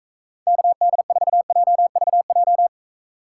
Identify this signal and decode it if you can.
Morse code, heard as a keyed tone, about 45 wpm, 700 Hz. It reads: KD4JVJ